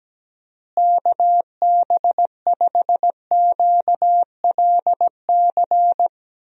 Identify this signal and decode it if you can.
Morse code, heard as a keyed tone, about 17 wpm, 700 Hz. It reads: KB5QLC
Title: USA Call Signs - 6 Characters - Repeated 2x - 17wpm